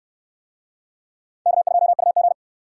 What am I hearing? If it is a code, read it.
Morse code, heard as a keyed tone, about 60 wpm, 700 Hz. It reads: K2UG